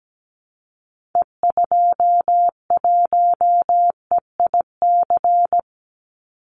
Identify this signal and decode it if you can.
Morse code, heard as a keyed tone, about 17 wpm, 700 Hz. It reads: E21EIC
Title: Call Signs Repeated 3x - 17wpm